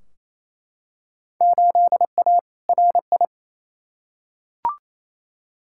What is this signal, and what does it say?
Morse code, heard as a keyed tone, about 28 wpm, 700 Hz. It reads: 8A RI
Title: Field Day Exchanges Encore - 28wpm